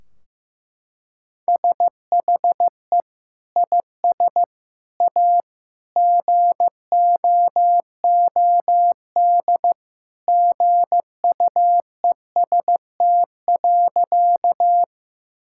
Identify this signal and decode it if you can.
Morse code, heard as a keyed tone, about 15 wpm, 700 Hz. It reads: SHE IS A GOOD GUEST.
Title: Sentences from Top 1600 Words 15wpm